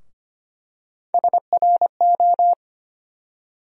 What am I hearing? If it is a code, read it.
Morse code, heard as a keyed tone, about 25 wpm, 700 Hz. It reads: SRO
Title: USA Call Sign Postfixes - 3 Letters 25wpm